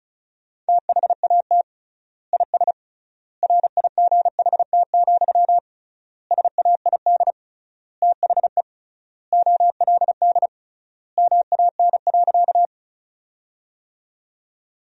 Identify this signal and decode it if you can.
Morse code, heard as a keyed tone, about 36 wpm, 700 Hz. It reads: THAT IS RIGHT, SAID THE OLD MAN.
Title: 3x Word-Spacing - Sentences from Top 200 Words - 35wpm